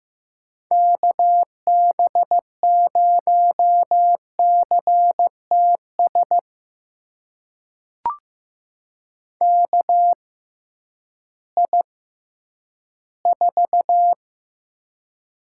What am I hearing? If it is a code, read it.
Morse code, heard as a keyed tone, about 15 wpm, spaced as for 10 wpm, 700 Hz. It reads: KB0CTS KI4
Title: USA Call Signs - 6 Characters - 6x Character Spacing - 15wpm